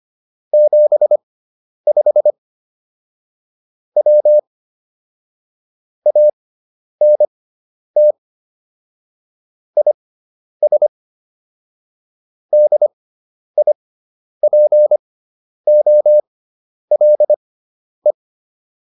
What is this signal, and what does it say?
Morse code, heard as a keyed tone, about 25 wpm, spaced as for 10 wpm, 600 Hz. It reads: 75 W ANT IS DIPOLE